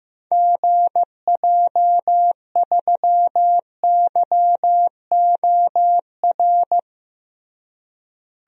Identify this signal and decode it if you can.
Morse code, heard as a keyed tone, about 15 wpm, 700 Hz. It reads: GJ3YOR
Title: DXCC-100 Call Signs - 6 Characters - Repeated 3x - 15wpm